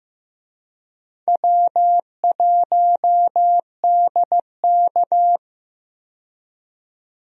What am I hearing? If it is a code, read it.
Morse code, heard as a keyed tone, about 15 wpm, 700 Hz. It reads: W1DK
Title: USA Call Signs - 4 Characters - 15wpm